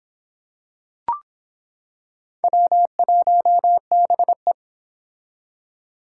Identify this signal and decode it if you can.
Morse code, heard as a keyed tone, about 25 wpm, 700 Hz. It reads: W1BE